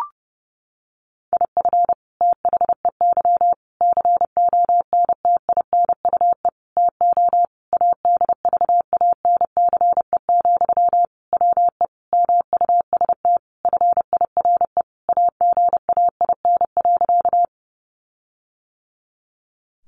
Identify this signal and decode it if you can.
Morse code, heard as a keyed tone, about 30 wpm, 700 Hz. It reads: IF THEY CONTINUE TO ADVANCE, WE MUST FIRE AGAIN.
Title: Sentences from Top 1400 Words 30wpm